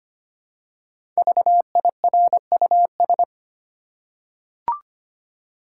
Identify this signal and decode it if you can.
Morse code, heard as a keyed tone, about 25 wpm, 700 Hz. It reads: VIRUS
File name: 5 Letter Words - First 3 Letters Given - 25wpm